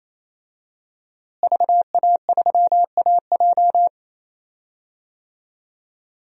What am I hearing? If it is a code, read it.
Morse code, heard as a keyed tone, about 27 wpm, 700 Hz. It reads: VA3AJ
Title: DXCC-100 Call Signs - 5 Characters - Repeated 3x - 28wpm